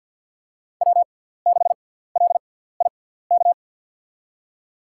Morse code, heard as a keyed tone, about 46 wpm, spaced as for 17 wpm, 700 Hz. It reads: W6LIK
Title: USA Call Signs - 5 Characters - 6x Character Spacing - 50wpm